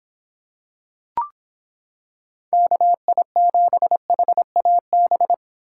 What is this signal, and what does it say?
Morse code, heard as a keyed tone, about 26 wpm, 700 Hz. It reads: KI7HAB